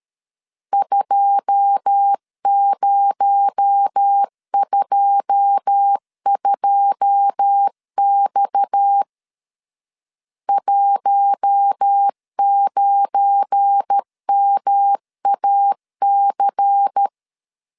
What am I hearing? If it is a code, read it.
Morse code, heard as a keyed tone, about 13 wpm, 800 Hz. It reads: 2022X 19MAC
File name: LEZIONI / 9 2 12wpm 2301